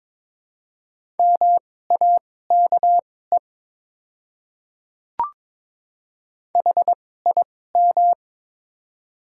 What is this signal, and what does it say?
Morse code, heard as a keyed tone, about 21 wpm, spaced as for 16 wpm, 700 Hz. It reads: MAKE HIM
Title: 2x Character Spacing - Top 100 Words - 22wpm